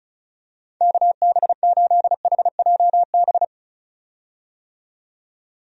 Morse code, heard as a keyed tone, about 36 wpm, 700 Hz. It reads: KB8HJB
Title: USA Call Signs - 6 Characters - 35wpm